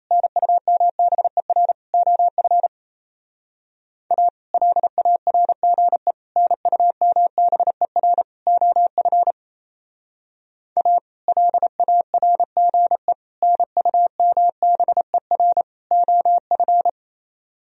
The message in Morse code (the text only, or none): NUMBER OF A LARGE NUMBER OF A LARGE NUMBER OF